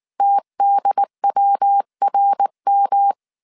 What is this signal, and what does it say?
Morse code, heard as a keyed tone, about 19 wpm, 800 Hz. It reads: TDWRM
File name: CW / DET / 1 3 1500C S B 18wpm 1916